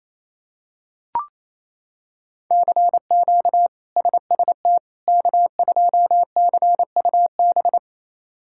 Morse code, heard as a keyed tone, about 28 wpm, 700 Hz. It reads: CQ SST K2CUB
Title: K1USN SST QSO Practice - 28wpm